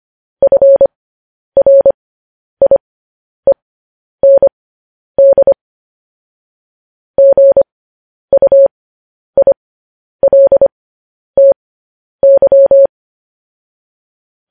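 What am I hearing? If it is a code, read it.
Morse code, heard as a keyed tone, about 25 wpm, spaced as for 10 wpm, 550 Hz. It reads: FRIEND GUILTY
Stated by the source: Morse code transmission of news headlines.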